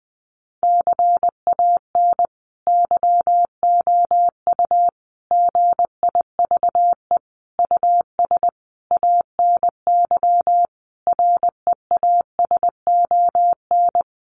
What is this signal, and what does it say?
Morse code, heard as a keyed tone, about 20 wpm, 700 Hz. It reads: CAN YOU GIVE US ANY REASON
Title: Sentences from Top 1100 Words 20wpm